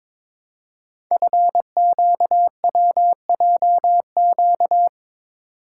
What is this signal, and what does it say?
Morse code, heard as a keyed tone, about 22 wpm, 700 Hz. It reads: FQWJQ
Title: Code Groups - 5 Letters - 22wpm